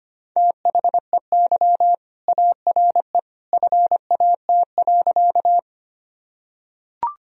Morse code, heard as a keyed tone, about 25 wpm, 700 Hz. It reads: THEY ARE FAT.
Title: Sentences from Top 1000 Words - Review Part 4 - 25wpm